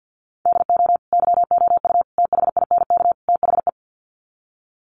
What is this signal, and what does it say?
Morse code, heard as a keyed tone, about 50 wpm, 700 Hz. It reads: DO YOU THINK THE
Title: Sets of 4 Words 50wpm